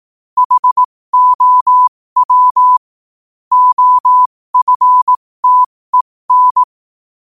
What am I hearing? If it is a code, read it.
Morse code, heard as a keyed tone, about 18 wpm, spaced as for 15 wpm, 1000 Hz. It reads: HOW OFTEN
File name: qotdfiles / 15wpm